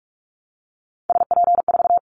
Morse code, heard as a keyed tone, about 46 wpm, 700 Hz. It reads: SP4
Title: Callsign Prefixes for Top 25 DXCC - 45wpm